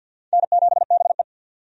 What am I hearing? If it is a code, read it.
Morse code, heard as a keyed tone, about 52 wpm, 700 Hz. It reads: N7BE
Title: USA Call Signs - 4 Characters - 50wpm